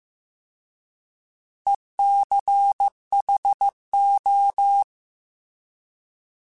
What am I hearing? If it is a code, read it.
Morse code, heard as a keyed tone, about 15 wpm, 800 Hz. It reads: ECHO
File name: Alphabetical / Alphabetical mp3s / Es @13